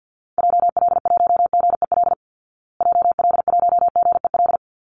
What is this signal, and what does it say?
Morse code, heard as a keyed tone, about 50 wpm, 700 Hz. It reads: JL1GEL JL1GEL
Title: Call Signs Repeated 3x - 50wpm